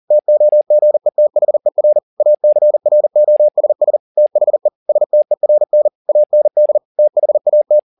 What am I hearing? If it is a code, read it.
Morse code, heard as a keyed tone, about 40 wpm, 600 Hz. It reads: TOGETHER ACROSS THE STERN AND THAT
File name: the old man and the sea 2 40wpm